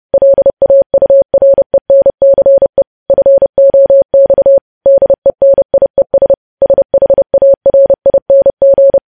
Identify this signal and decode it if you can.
Morse code, heard as a keyed tone, about 30 wpm, 550 Hz. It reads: LAURENCE FOX DENIES SHARING
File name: podcast / News headlines-1x-30x30